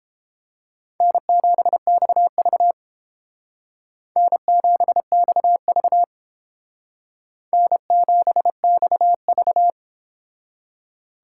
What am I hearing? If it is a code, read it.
Morse code, heard as a keyed tone, about 30 wpm, 700 Hz. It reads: N7XV N7XV N7XV